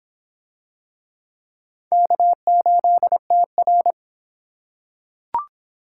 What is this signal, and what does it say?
Morse code, heard as a keyed tone, about 25 wpm, 700 Hz. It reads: K8TR